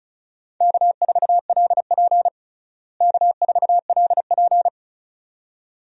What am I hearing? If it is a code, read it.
Morse code, heard as a keyed tone, about 34 wpm, 700 Hz. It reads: K4LP K4LP
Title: USA Call Signs - 4 Characters - Repeated 2x - 35wpm